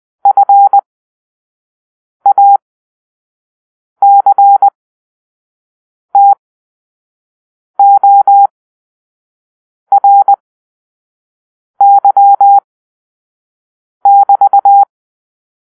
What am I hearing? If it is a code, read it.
Morse code, heard as a keyed tone, about 20 wpm, spaced as for 5 wpm, 800 Hz. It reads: FACTORY<BT>
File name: arrl / Grp-4 Drill-4